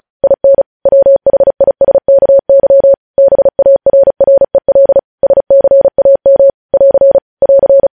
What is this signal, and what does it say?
Morse code, heard as a keyed tone, about 36 wpm, 550 Hz. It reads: IN WHISKY BARREL SCAM <AR> <AR>
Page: News headlines in Morse code at 35 wpm